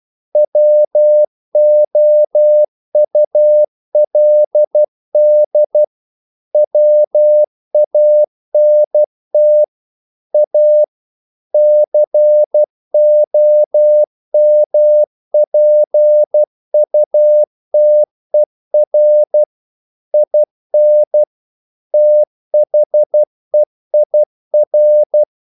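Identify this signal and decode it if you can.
Morse code, heard as a keyed tone, about 12 wpm, 600 Hz. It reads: WOULD WANT A COMPUTER IN THEIR
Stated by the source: Morse practice sending English quotations.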